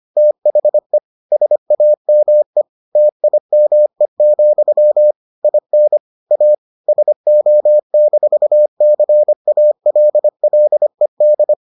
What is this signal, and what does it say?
Morse code, heard as a keyed tone, about 25 wpm, 600 Hz. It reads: THE SAME TIME, IN A SO-CALLED